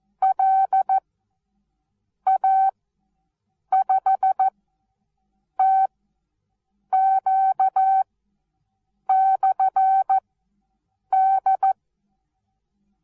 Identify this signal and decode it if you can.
Morse code, heard as a keyed tone, about 14 wpm, spaced as for 6 wpm, 770 Hz. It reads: LA5TQ/D
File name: Onlinelehrgang / morsen / DJ4UF Morsen19